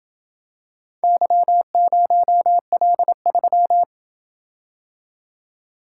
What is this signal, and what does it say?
Morse code, heard as a keyed tone, about 27 wpm, 700 Hz. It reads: Y0L3